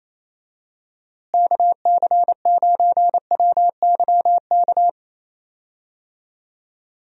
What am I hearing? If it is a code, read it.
Morse code, heard as a keyed tone, about 28 wpm, 700 Hz. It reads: KC9WYK
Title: USA Call Signs - 6 Characters - Repeated 2x - 28wpm